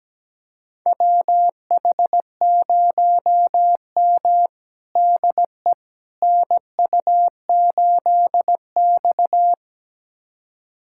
Morse code, heard as a keyed tone, about 17 wpm, 700 Hz. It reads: WH0M DE NU8X
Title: USA Callsign DE Callsign - 4 Characters - 17wpm